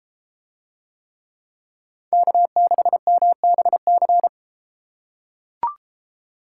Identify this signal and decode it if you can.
Morse code, heard as a keyed tone, about 33 wpm, 700 Hz. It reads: K6MBC